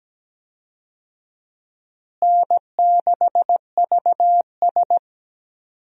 Morse code, heard as a keyed tone, about 17 wpm, 700 Hz. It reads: N6VS
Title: USA Call Signs - 4 Characters - Repeated 3x - 17wpm